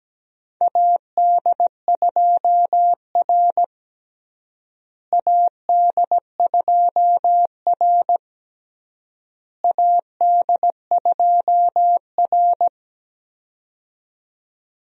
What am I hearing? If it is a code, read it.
Morse code, heard as a keyed tone, about 17 wpm, 700 Hz. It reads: AD2R AD2R AD2R
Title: USA Call Signs - 4 Characters - Repeated 3x - 17wpm